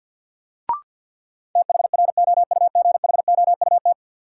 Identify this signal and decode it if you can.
Morse code, heard as a keyed tone, about 52 wpm, 700 Hz. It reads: THROUGHOUT
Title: Sentences from Top 800 Words - New Word Drill - 50wpm